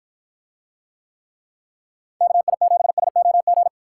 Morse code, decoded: KI7SOG

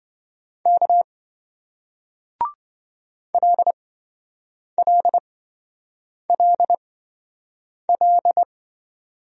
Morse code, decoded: K LLLL

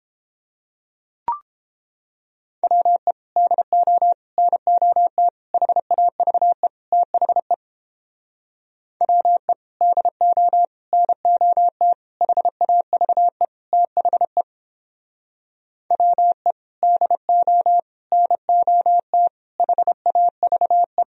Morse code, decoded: WE DO NOT HAVE THE WE DO NOT HAVE THE WE DO NOT HAVE

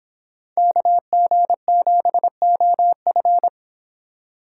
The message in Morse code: KG7OF

